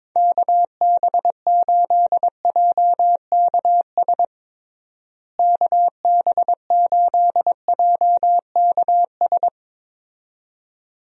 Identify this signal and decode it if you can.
Morse code, heard as a keyed tone, about 22 wpm, 700 Hz. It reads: KB8JKS KB8JKS
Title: USA Call Signs - 6 Characters - Repeated 2x - 22wpm